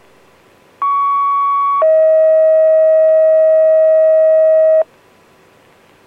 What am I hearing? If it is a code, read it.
Two-tone page, heard as a one-second tone, then a three-second tone, about 1120 then 640 Hz. It is heard from 0.8 to 4.8 s.